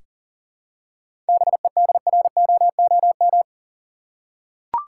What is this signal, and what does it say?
Morse code, heard as a keyed tone, about 38 wpm, 700 Hz. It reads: BEDROOM
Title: Seven Letter Words - 40wpm